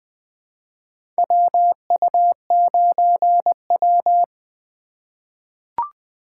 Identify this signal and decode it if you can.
Morse code, heard as a keyed tone, about 20 wpm, 700 Hz. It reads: WU9W
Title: USA Call Signs - 4 Characters - 3x Character Spacing - 20wpm